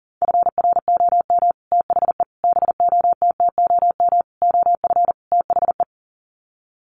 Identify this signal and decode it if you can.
Morse code, heard as a keyed tone, about 40 wpm, 700 Hz. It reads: FROM THE BOTTOM OF THE
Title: Sets of 5 Words Encore2 40wpm